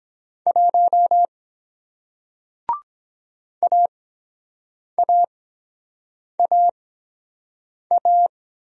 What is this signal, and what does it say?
Morse code, heard as a keyed tone, about 23 wpm, spaced as for 7 wpm, 700 Hz. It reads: 1 AAAA